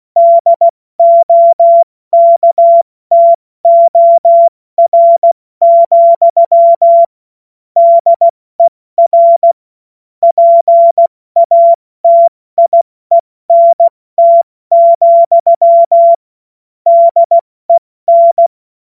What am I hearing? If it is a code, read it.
Morse code, heard as a keyed tone, about 16 wpm, 680 Hz. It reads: DOKTOR, DER PATIENT, DEN
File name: Morsekurs / Witze / Witz 04. Juli 2023 70 F80